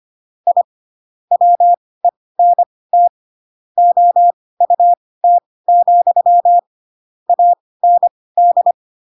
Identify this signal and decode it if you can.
Morse code, heard as a keyed tone, about 25 wpm, spaced as for 18 wpm, 700 Hz. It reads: I WENT OUT, AND
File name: Pinocchio / Pinocchio ch7 12 25 ik1hge